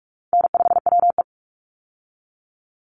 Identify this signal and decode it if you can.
Morse code, heard as a keyed tone, about 44 wpm, 700 Hz. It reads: N5WE
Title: Easy Call Signs - 45wpm